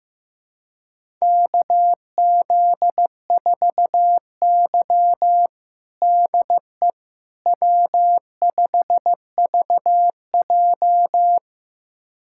Morse code, heard as a keyed tone, about 15 wpm, 700 Hz. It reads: KZ4Y DE W5VJ
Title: USA Callsign DE Callsign - 4 Characters - 15wpm